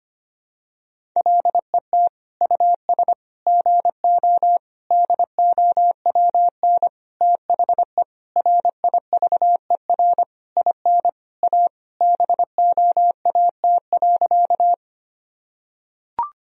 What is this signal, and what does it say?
Morse code, heard as a keyed tone, about 25 wpm, 700 Hz. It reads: LET US GO DOWN THE RIVER IN A BOAT.
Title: Sentences from Top 1000 Words - Review Part 3 - 25wpm